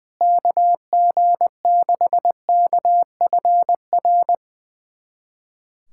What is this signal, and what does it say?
Morse code, heard as a keyed tone, about 21 wpm, 700 Hz. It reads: KG6KFR
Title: USA Callsign DE Callsign - 6 Characters - 20wpm